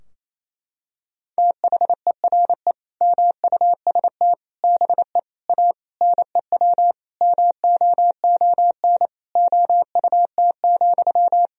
Morse code, heard as a keyed tone, about 27 wpm, 690 Hz. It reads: THERE MUST BE A NEW MOON OUT,
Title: Sentences from Top 1000 Words - Review Part 3 - 28wpm